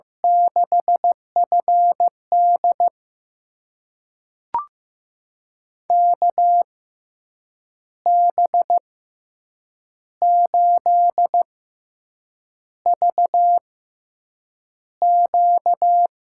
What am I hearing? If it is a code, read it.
Morse code, heard as a keyed tone, about 15 wpm, spaced as for 7 wpm, 700 Hz. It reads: 6FD KB8VQ
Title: USA Call Signs - 6x Character Spacing - 15wpm